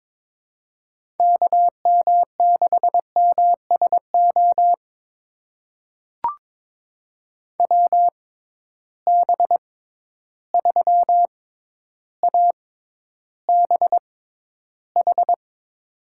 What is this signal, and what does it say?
Morse code, heard as a keyed tone, about 22 wpm, spaced as for 11 wpm, 700 Hz. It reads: KM6MSO WB3ABH
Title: USA Call Signs - 6 Characters - 6x Character Spacing - 22wpm